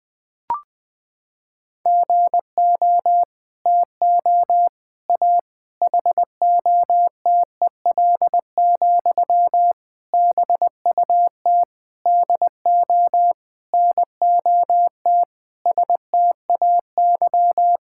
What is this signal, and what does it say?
Morse code, heard as a keyed tone, about 19 wpm, 700 Hz. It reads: GO TO A HOTEL, BUT DO NOT STAY
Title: Sentences from Top 1600 Words 20wpm